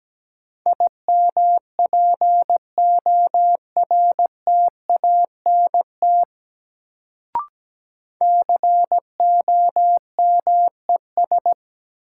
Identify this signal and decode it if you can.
Morse code, heard as a keyed tone, about 17 wpm, 700 Hz. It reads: IMPORTANT COMES
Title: Sets of 3 Words - New Word Drill - 17wpm